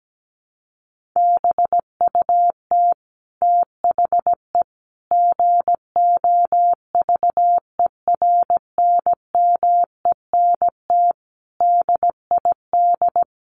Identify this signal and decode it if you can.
Morse code, heard as a keyed tone, about 17 wpm, 700 Hz. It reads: BUT THE GOVERNMENT DID
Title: Sentences from Top 1400 Words 17wpm